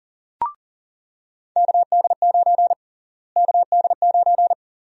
Morse code, heard as a keyed tone, about 40 wpm, 700 Hz. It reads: KD9 KD9
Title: USA Call Sign Prefixes - Repeated 2x - 40wpm